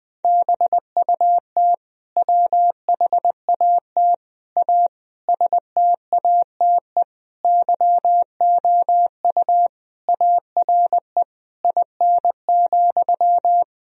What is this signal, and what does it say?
Morse code, heard as a keyed tone, about 21 wpm, 700 Hz. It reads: BUT WHAT A STATE YOU ARE IN,
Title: Sentences from Top 1000 Words - Review Part 2 - 20wpm